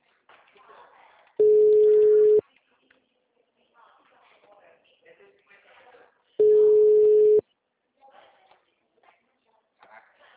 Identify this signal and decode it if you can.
Ringback tone (Europe, 425 Hz), with speech.